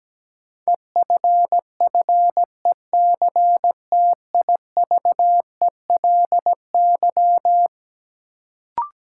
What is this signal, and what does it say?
Morse code, heard as a keyed tone, about 17 wpm, 700 Hz. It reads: EFFECTIVELY
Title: Sentences from Top 1400 Words - New Word Drill - 17wpm